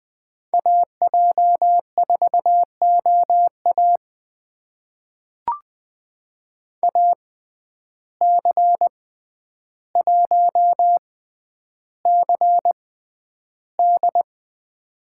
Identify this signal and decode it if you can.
Morse code, heard as a keyed tone, about 21 wpm, spaced as for 10 wpm, 700 Hz. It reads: AJ4OA AC1CD
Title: USA Call Signs - 5 Characters - 6x Character Spacing - 20wpm